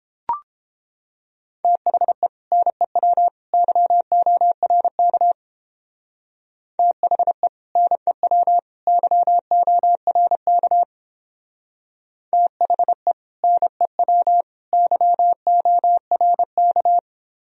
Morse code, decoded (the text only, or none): THE NEW YORK THE NEW YORK THE NEW YORK